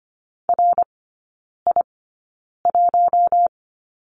Morse code, heard as a keyed tone, about 25 wpm, spaced as for 9 wpm, 690 Hz. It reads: RI1
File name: Lesson 14e - Number 2 - Three Characters 25wpm